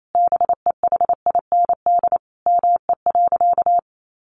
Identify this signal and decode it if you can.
Morse code, heard as a keyed tone, about 28 wpm, 700 Hz. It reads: BEHIND ME.
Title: Sentences from Top 500 Words 28wpm